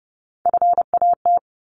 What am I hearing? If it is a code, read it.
Morse code, heard as a keyed tone, about 30 wpm, 700 Hz. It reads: FAT
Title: Top 501-750 Words 30wpm